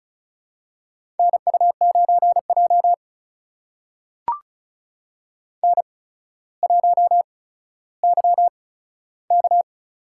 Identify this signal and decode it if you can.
Morse code, heard as a keyed tone, about 36 wpm, spaced as for 15 wpm, 700 Hz. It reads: NU9J N1YK
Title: USA Call Signs - 4 Characters - 8x Character Spacing - 35wpm